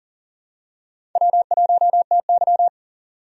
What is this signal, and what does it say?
Morse code, heard as a keyed tone, about 40 wpm, 700 Hz. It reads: W1TY
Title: USA Call Signs - 4 Characters - 40wpm